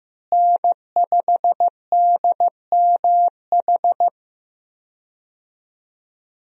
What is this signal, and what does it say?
Morse code, heard as a keyed tone, about 16 wpm, 700 Hz. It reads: N5DMH
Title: USA Call Signs - 5 Characters - Repeated 2x - 15wpm